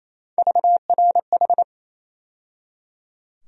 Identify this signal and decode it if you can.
Morse code, heard as a keyed tone, about 28 wpm, 700 Hz. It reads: VRH
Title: USA Call Sign Postfixes - 3 Letters 28wpm